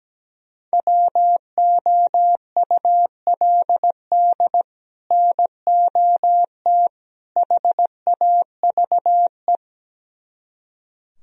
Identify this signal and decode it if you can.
Morse code, heard as a keyed tone, about 17 wpm, 700 Hz. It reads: WOULD NOT HAVE
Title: Speed-Racing - Sets of 3 Words 17wpm